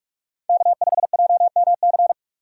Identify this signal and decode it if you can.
Morse code, heard as a keyed tone, about 46 wpm, 700 Hz. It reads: K5JMC
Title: USA Call Signs - 5 Characters - 4x Character Spacing - 45wpm